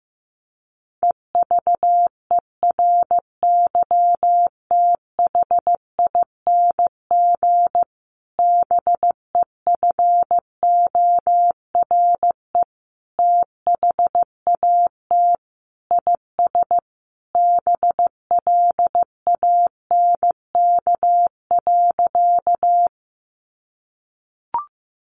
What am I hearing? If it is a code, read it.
Morse code, heard as a keyed tone, about 15 wpm, 700 Hz. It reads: EVERYTHING BEFORE THAT IS BLANK.
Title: Sentences from Top 1500 Words 15wpm